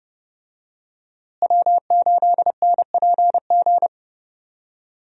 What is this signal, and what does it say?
Morse code, heard as a keyed tone, about 30 wpm, 700 Hz. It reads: W8NPG
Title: USA Call Signs - 5 Characters - Repeated 3x - 30wpm